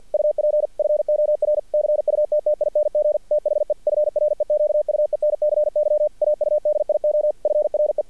Morse code, heard as a keyed tone, about 50 wpm, 600 Hz. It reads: UP FOR CUTTING THE FREQUENCY AUDIO FRE
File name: archived / CRSnet